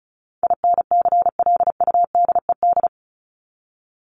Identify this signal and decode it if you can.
Morse code, heard as a keyed tone, about 34 wpm, 700 Hz. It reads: INCLUDED
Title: ED Suffix - Top 100 Words with Base Words 35wpm